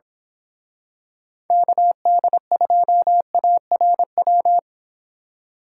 Morse code, heard as a keyed tone, about 26 wpm, 700 Hz. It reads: KD2ARW
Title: Speed Racing - USA 2x3 Call Signs 22wpm